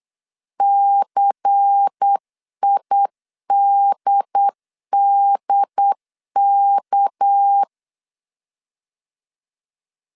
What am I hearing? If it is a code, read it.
Morse code, heard as a keyed tone, about 8 wpm, 800 Hz. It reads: CIDDK
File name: 8wpm